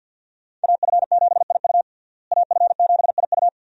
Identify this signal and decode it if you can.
Morse code, heard as a keyed tone, about 47 wpm, 700 Hz. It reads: AF7IU AF7IU
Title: USA Call Signs - 5 Characters - Repeated 2x - 50wpm